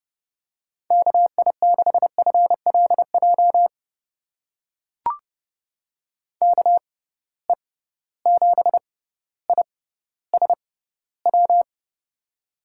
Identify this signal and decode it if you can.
Morse code, heard as a keyed tone, about 30 wpm, spaced as for 15 wpm, 700 Hz. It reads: KI6FLJ KE7ISW
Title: USA Call Signs - 6x Character Spacing - 30wpm